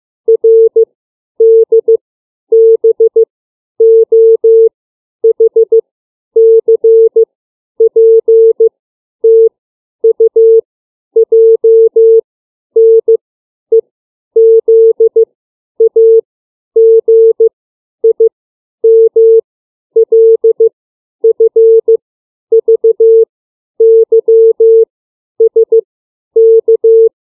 Morse code, heard as a keyed tone, about 15 wpm, 450 Hz. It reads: R D B O H C P T U J N E Z A G I M L F V Y S K
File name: CW / alphabet 9